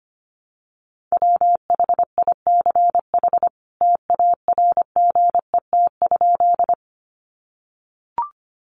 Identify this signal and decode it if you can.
Morse code, heard as a keyed tone, about 25 wpm, 700 Hz. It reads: WHICH TARGET?
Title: Sentences from Top 1200 Words 25wpm